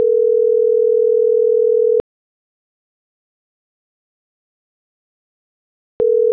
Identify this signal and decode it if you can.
Ringback tone (North America), 440 + 480 Hz.